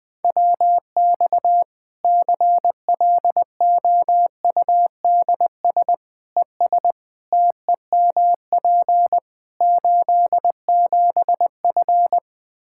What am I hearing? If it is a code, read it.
Morse code, heard as a keyed tone, about 20 wpm, 700 Hz. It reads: WX CLOUDS ES TEMP 87F
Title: Practice QSO - Exchange 2 - Info - 20wpm